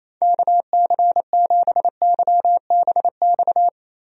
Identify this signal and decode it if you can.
Morse code, heard as a keyed tone, about 27 wpm, 700 Hz. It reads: KC7YBX
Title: USA Call Signs - 6 Characters - Repeated 3x - 28wpm